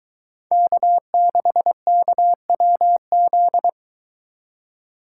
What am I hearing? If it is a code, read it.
Morse code, heard as a keyed tone, about 23 wpm, 700 Hz. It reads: K6KWZ